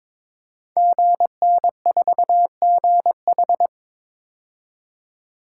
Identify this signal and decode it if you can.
Morse code, heard as a keyed tone, about 22 wpm, 700 Hz. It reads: GN4GH
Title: Code Groups - 5 Letter:Numbers - 22wpm